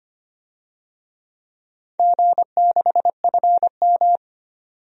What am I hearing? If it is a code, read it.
Morse code, heard as a keyed tone, about 25 wpm, 700 Hz. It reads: G6FM
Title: DXCC-100 Call Signs - 4 Characters - 25wpm